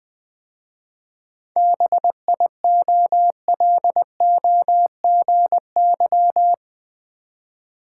Morse code, heard as a keyed tone, about 20 wpm, 690 Hz. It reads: BIOLOGY
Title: 7 Letter Words - First 3 Letters Given - 20wpm